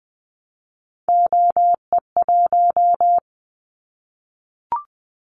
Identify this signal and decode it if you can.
Morse code, heard as a keyed tone, about 20 wpm, 700 Hz. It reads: OE1
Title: Callsign Prefixes for Top 100 DXCC - 20wpm